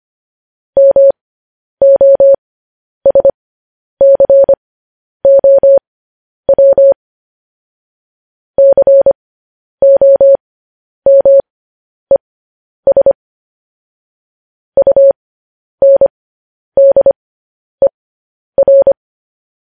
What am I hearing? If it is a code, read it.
Morse code, heard as a keyed tone, about 25 wpm, spaced as for 10 wpm, 550 Hz. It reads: MOSCOW COMES UNDER
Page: news-headlines-in-morse-code-at-10-wpm